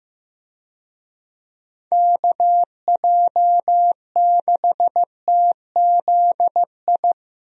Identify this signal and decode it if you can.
Morse code, heard as a keyed tone, about 15 wpm, 700 Hz. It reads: KJ6TZI
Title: USA Call Signs - 6 Characters - Repeated 2x - 15wpm